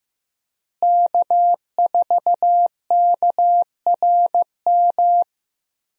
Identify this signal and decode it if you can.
Morse code, heard as a keyed tone, about 15 wpm, 700 Hz. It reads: K4KRM
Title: USA Call Signs - 5 Characters - Repeated 3x - 15wpm